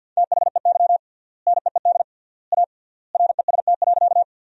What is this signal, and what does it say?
Morse code, heard as a keyed tone, about 52 wpm, 700 Hz. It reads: THEY NEED A REST.
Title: 3x Word-Spacing - Sentences from Top 500 Words - 50wpm